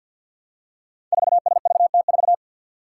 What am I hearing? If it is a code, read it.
Morse code, heard as a keyed tone, about 50 wpm, 700 Hz. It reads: 4SVT4
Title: Five Challenging Letter-Number Combinations 50wpm